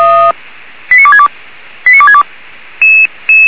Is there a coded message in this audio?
Yes: five-tone selective call (ZVEI) from 0.9 to 2.2 s, heard as quick runs of five tones, tones of about 70 ms. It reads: 89262 89262